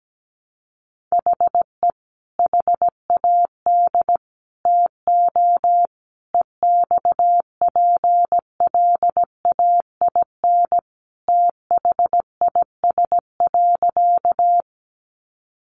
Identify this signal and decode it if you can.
Morse code, heard as a keyed tone, about 17 wpm, 700 Hz. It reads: HE HAD TO EXPLAIN THIS.
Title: Sentences from Top 1100 Words 17wpm